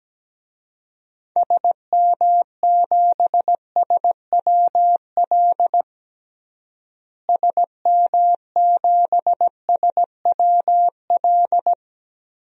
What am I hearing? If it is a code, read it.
Morse code, heard as a keyed tone, about 17 wpm, 700 Hz. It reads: SM7SWL SM7SWL
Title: DXCC-100 Call Signs - 6 Characters - Repeated 2x - 17wpm